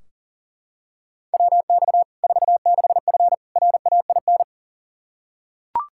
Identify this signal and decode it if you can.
Morse code, heard as a keyed tone, about 37 wpm, 700 Hz. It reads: WX 46F RAIN
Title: Practice QSO - Exchange 2 - Info - 40wpm